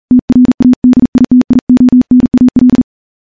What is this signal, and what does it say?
Morse code, heard as a keyed tone, about 38 wpm, 250 Hz. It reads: TRADITIONAL
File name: morse-training-40WPM